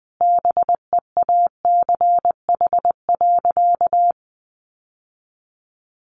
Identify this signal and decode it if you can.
Morse code, heard as a keyed tone, about 19 wpm, 700 Hz. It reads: BEACH.
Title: Sentences from Top 1500 Words 20wpm